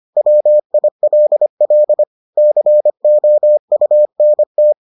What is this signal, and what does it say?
Morse code, heard as a keyed tone, about 25 wpm, 600 Hz. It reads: WILL COUNT